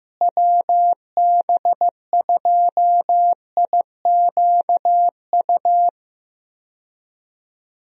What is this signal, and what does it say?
Morse code, heard as a keyed tone, about 15 wpm, 700 Hz. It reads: WB2IQU
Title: USA Call Signs - 6 Characters - Repeated 3x - 15wpm